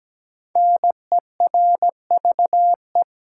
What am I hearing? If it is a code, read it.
Morse code, heard as a keyed tone, about 17 wpm, 700 Hz. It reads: NERVE
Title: Five Letter Words - 17wpm